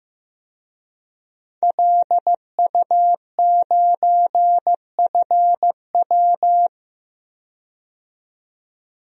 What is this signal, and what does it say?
Morse code, heard as a keyed tone, about 15 wpm, 700 Hz. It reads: LU9FW
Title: DXCC-100 Call Signs - 5 Characters - 15wpm